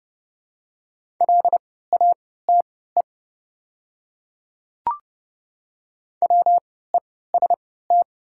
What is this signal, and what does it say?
Morse code, heard as a keyed tone, about 29 wpm, spaced as for 17 wpm, 690 Hz. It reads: LATE WEST
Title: 3x Character Spacing - Top 500 Words - 30wpm